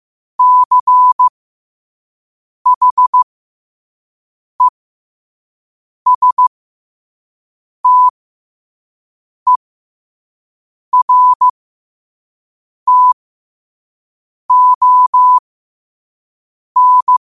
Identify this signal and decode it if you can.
Morse code, heard as a keyed tone, about 15 wpm, spaced as for 5 wpm, 1000 Hz. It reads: CHESTERTON
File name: qotdfiles / 5wpm